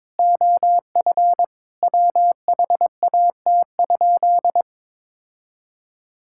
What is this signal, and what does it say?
Morse code, heard as a keyed tone, about 22 wpm, 700 Hz. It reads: OF WHAT?
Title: Sentences from Top 1000 Words - Review Part 5 - 22wpm